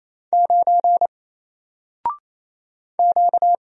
Morse code, heard as a keyed tone, about 28 wpm, 700 Hz. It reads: 9 Q
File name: Single-Letter-Number-28wpm